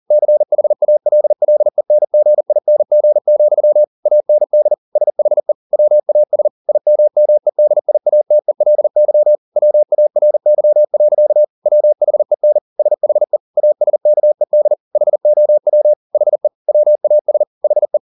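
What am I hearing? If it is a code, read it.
Morse code, heard as a keyed tone, about 40 wpm, 600 Hz. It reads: CHALLENGING, AND SHE WAS IMMEDIATELY WARY. WHEN SHE ASKED HOW HE WAS HE
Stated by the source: Morse code at 40 wpm, practice text, CW med Gnister.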